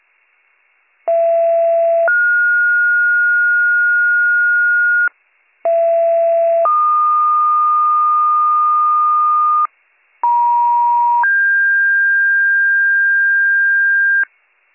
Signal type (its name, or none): two-tone page